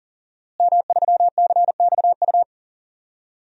Morse code, heard as a keyed tone, about 37 wpm, 700 Hz. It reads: M3CXU
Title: DXCC-100 Call Signs - 5 Characters - Repeated 3x - 40wpm